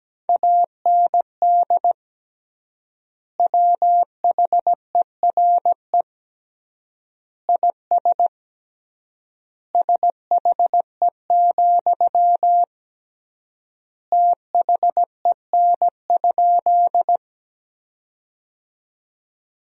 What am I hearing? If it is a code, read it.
Morse code, heard as a keyed tone, about 17 wpm, 700 Hz. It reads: AND WHERE IS SHE, THEN?
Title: 3x Word-Spacing - Sentences from Top 200 Words - 17wpm